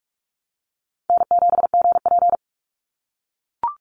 Morse code, heard as a keyed tone, about 44 wpm, 700 Hz. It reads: N7GP
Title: USA Call Signs 45wpm